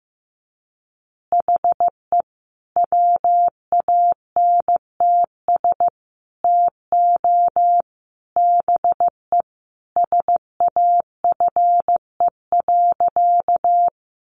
Morse code, heard as a keyed tone, about 15 wpm, 700 Hz. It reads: HE WANTS TO BE SAFE.
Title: Additional Easy Phrases 15wpm